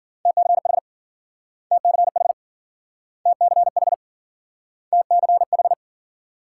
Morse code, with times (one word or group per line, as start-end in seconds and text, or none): 0.3-0.8 TCH
1.7-2.3 TCH
3.3-3.9 TCH
4.9-5.7 TCH